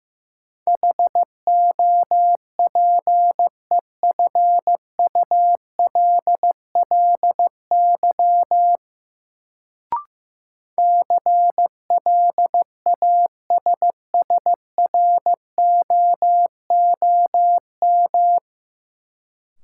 Morse code, heard as a keyed tone, about 15 wpm, 700 Hz. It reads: HOPEFULLY CLASSROOM
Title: Sentences from Top 1500 Words - New Word Drill - 15wpm